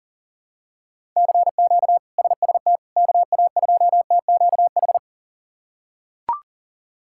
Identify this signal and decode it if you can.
Morse code, heard as a keyed tone, about 40 wpm, 700 Hz. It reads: CQ SST KA2TQH
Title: K1USN SST QSO Practice - 40wpm